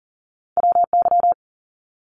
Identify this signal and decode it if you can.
Morse code, heard as a keyed tone, about 39 wpm, 700 Hz. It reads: WY
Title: US States 40wpm